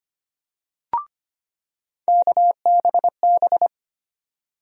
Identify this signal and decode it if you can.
Morse code, heard as a keyed tone, about 25 wpm, 700 Hz. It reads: KBB
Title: USA Call Sign Postfixes - 3 Letters 25wpm